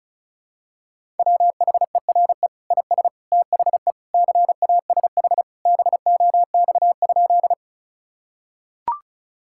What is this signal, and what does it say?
Morse code, heard as a keyed tone, about 36 wpm, 700 Hz. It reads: WHERE IS THE CASH BOX?
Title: Sentences from Top 1000 Words - Review Part 5 - 35wpm